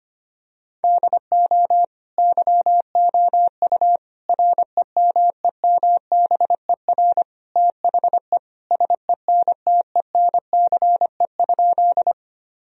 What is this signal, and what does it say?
Morse code, heard as a keyed tone, about 25 wpm, 700 Hz. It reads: DO YOU REMEMBER THE SENTENCE?
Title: Sentences from Top 1000 Words - Review Part 1 - 25wpm